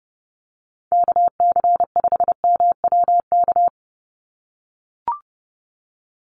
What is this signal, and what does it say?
Morse code, heard as a keyed tone, about 30 wpm, 700 Hz. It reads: KC5MWK
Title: USA Call Signs 30wpm